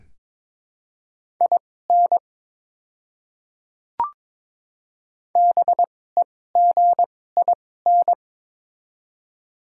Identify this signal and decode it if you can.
Morse code, heard as a keyed tone, about 22 wpm, spaced as for 16 wpm, 700 Hz. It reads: IN BEGIN
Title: 2x Character Spacing - Top 300 Words - 22wpm